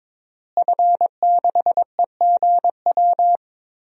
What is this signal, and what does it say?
Morse code, heard as a keyed tone, about 22 wpm, 700 Hz. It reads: F6EGW
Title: DXCC-100 Call Signs - 5 Characters - Repeated 3x - 22wpm